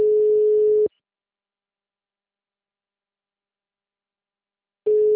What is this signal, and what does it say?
Ringback tone (Europe, 425 Hz).